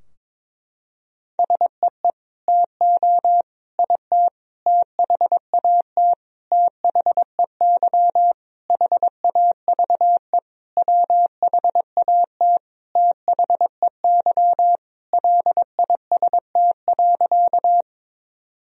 Morse code, heard as a keyed tone, about 22 wpm, 700 Hz. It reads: SEE TO IT THAT THEY HAVE WHAT THEY LIST.
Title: Sentences from Top 1000 Words - Review Part 2 - 22wpm